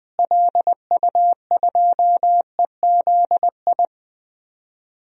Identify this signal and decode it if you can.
Morse code, heard as a keyed tone, about 20 wpm, 700 Hz. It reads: LU2EZI
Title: DXCC-100 Call Signs - 6 Characters - Repeated 2x - 20wpm